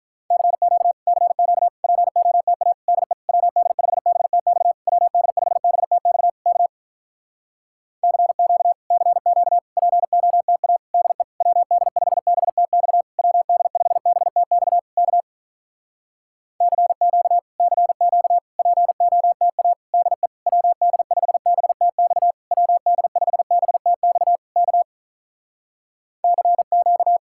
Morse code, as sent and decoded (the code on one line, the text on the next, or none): -.-. --.- / -.-. --.- / .--. --- - .- / -.. / .-- -.. ..... -... - -..- / .-- -.. ..... -... - -..- / -.- / -.-. --.- / -.-. --.- / .--. --- - .- / -.. / .-- -.. ..... -... - -..- / .-- -.. ..... -... - -..- / -.- / -.-. --.- / -.-. --.- / .--. --- - .- / -.. / .-- -.. ..... -... - -..- / .-- -.. ..... -... - -..- / -.- / -.-. --.-
CQ CQ POTA DE WD5BTX WD5BTX K CQ CQ POTA DE WD5BTX WD5BTX K CQ CQ POTA DE WD5BTX WD5BTX K CQ